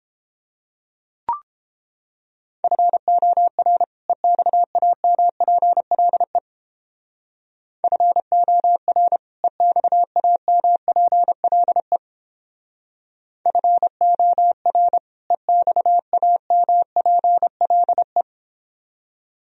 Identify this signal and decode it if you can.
Morse code, heard as a keyed tone, about 31 wpm, 700 Hz. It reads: FOR EXAMPLE FOR EXAMPLE FOR EXAMPLE